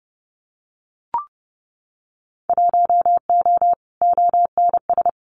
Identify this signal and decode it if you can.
Morse code, heard as a keyed tone, about 30 wpm, 700 Hz. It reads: 1O ONS
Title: Winter Field Day Exchanges Encore 30wpm